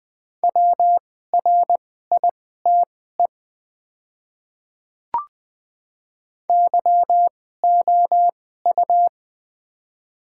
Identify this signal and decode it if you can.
Morse code, heard as a keyed tone, about 20 wpm, spaced as for 14 wpm, 700 Hz. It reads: WRITE YOU
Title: 2x Character Spacing - Top 100 Words - 20wpm